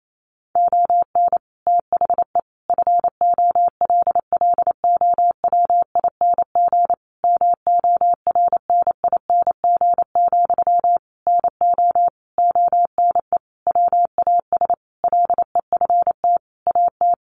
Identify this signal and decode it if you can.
Morse code, heard as a keyed tone, about 28 wpm, 700 Hz. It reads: ON THE FOLLOWING MORNING, NO ONE WAS LEFT AT
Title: Sentences from Top 1400 Words 28wpm